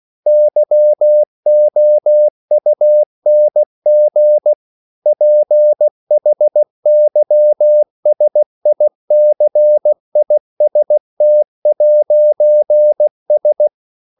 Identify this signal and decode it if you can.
Morse code, heard as a keyed tone, about 16 wpm, 600 Hz. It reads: YOUNG PHYSICIST'S
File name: CW / Practice / EnglishQuotations0416